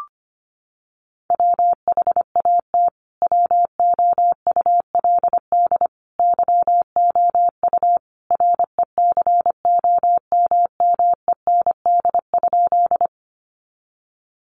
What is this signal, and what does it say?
Morse code, heard as a keyed tone, about 25 wpm, 700 Hz. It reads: WHAT WOULD YOU RECOMMEND?
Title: Sentences from Top 1200 Words 25wpm